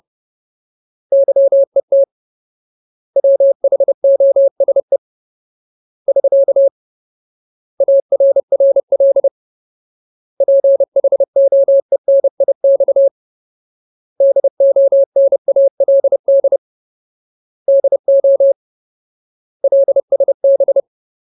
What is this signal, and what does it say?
Morse code, heard as a keyed tone, about 30 wpm, 550 Hz. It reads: YET WHOSE <SK> ARRL PHOENIX DONALD DO LSB
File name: Ultimate Word List / 30WPM Ultimate